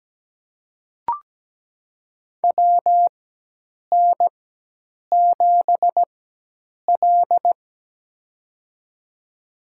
Morse code, heard as a keyed tone, about 17 wpm, spaced as for 8 wpm, 700 Hz. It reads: WN7L
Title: USA Call Signs - 4 Characters - 4x Character Spacing - 17wpm